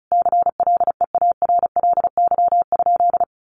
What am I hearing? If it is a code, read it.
Morse code, heard as a keyed tone, about 34 wpm, 700 Hz. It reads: CLEARLY?
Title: Sentences from Top 1300 Words 35wpm